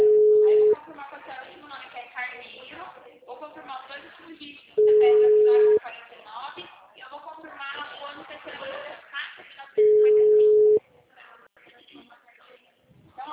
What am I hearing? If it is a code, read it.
Ringback tone (Europe, 425 Hz), with speech.